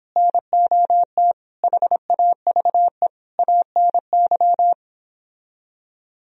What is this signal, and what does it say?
Morse code, heard as a keyed tone, about 31 wpm, 700 Hz. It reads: NOT HAVE ANY